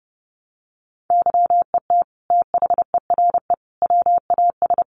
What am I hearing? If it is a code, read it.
Morse code, heard as a keyed tone, about 30 wpm, 700 Hz. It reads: YET THERE WAS
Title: Sentences from Top 1300 Words 30wpm